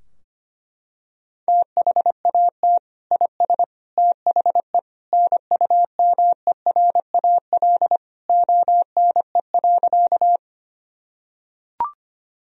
Morse code, heard as a keyed tone, about 25 wpm, 700 Hz. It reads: THAT IS THE NUMERAL ONE.
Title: Sentences from Top 1000 Words - Review Part 2 - 25wpm